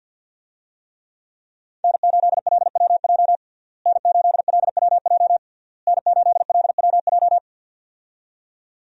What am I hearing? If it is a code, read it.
Morse code, heard as a keyed tone, about 52 wpm, 700 Hz. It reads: N8LWJ N8LWJ N8LWJ
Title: USA Call Signs - 5 Characters - Repeated 3x - 50wpm